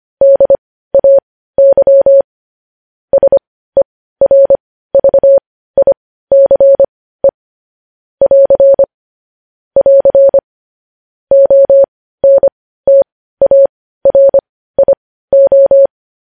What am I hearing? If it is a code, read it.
Morse code, heard as a keyed tone, about 25 wpm, spaced as for 15 wpm, 550 Hz. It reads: DAY SERVICE <AR> <AR> ONTARIO
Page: news headlines in morse code at 15 wpm